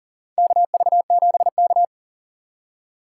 Morse code, heard as a keyed tone, about 40 wpm, 700 Hz. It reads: KV7K
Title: USA Call Signs - 4 Characters - 2x Character Spacing - 40wpm